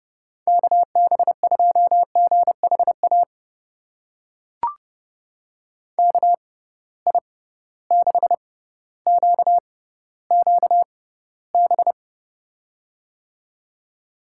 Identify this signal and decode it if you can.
Morse code, heard as a keyed tone, about 30 wpm, spaced as for 15 wpm, 700 Hz. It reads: KB2GHA KI6QQB
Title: USA Call Signs - 6 Characters - 6x Character Spacing - 30wpm